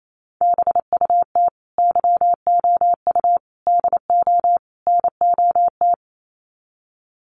Morse code, heard as a keyed tone, about 28 wpm, 700 Hz. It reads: BUT YOU DO NOT
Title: Sets of 4 Words 28wpm